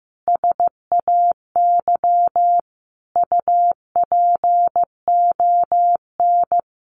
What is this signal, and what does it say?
Morse code, heard as a keyed tone, about 15 wpm, 700 Hz. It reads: SAY UPON
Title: Sentences from Top 1200 Words 15wpm